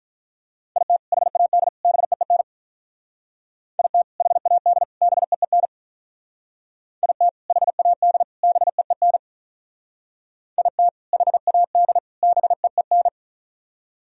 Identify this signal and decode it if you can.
Morse code, heard as a keyed tone, about 44 wpm, 700 Hz. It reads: IT HAD BEEN IT HAD BEEN IT HAD BEEN IT HAD BEEN